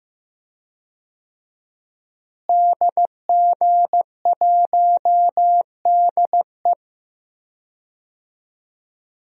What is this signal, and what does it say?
Morse code, heard as a keyed tone, about 16 wpm, 700 Hz. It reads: DG1DE
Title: DXCC-100 Call Signs - 5 Characters - Repeated 3x - 15wpm